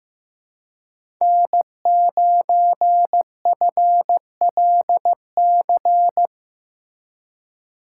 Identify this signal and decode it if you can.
Morse code, heard as a keyed tone, about 15 wpm, 700 Hz. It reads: N9FLC
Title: USA Call Signs - 5 Characters - 2x Character Spacing - 15wpm